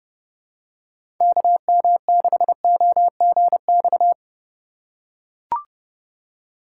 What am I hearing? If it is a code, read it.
Morse code, heard as a keyed tone, about 32 wpm, 700 Hz. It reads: KM6OGX
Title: USA Call Signs - 6 Characters - 2x Character Spacing - 30wpm